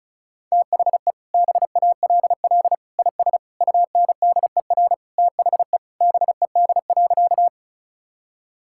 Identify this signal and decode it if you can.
Morse code, heard as a keyed tone, about 36 wpm, 700 Hz. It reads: THE BALL IS UNDER THE BED.
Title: Sentences from Top 1000 Words - Review Part 6 - 35wpm